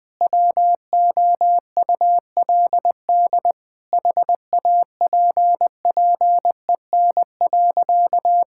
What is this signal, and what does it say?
Morse code, heard as a keyed tone, about 20 wpm, 700 Hz. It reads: WOULD HAPPEN.
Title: Sentences from Top 1000 Words - Review Part 6 - 20wpm